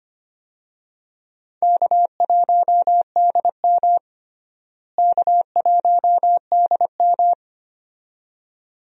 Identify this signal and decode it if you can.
Morse code, heard as a keyed tone, about 25 wpm, 700 Hz. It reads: K1DM K1DM
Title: USA Call Signs - 4 Characters - Repeated 2x - 25wpm